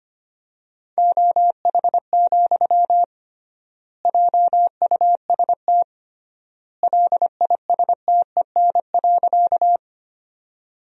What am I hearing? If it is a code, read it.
Morse code, heard as a keyed tone, about 25 wpm, 700 Hz. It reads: OH, JUST LISTEN.
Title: 3x Word-Spacing - Sentences from Top 500 Words - 25wpm